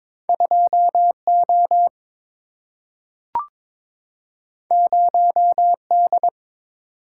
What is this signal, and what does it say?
Morse code, heard as a keyed tone, about 22 wpm, 700 Hz. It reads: 2 O 0 D
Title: Single-Letter Single-Number 22wpm